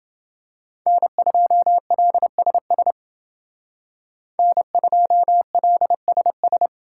Morse code, heard as a keyed tone, about 28 wpm, 700 Hz. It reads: N2LSS N2LSS